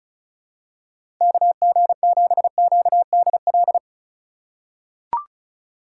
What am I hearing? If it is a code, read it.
Morse code, heard as a keyed tone, about 36 wpm, 700 Hz. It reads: KG7QDL
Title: USA Call Signs - 6 Characters - 2x Character Spacing - 35wpm